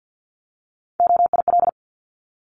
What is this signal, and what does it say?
Morse code, heard as a keyed tone, about 50 wpm, 700 Hz. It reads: OIL
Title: Top 501-750 Words 50wpm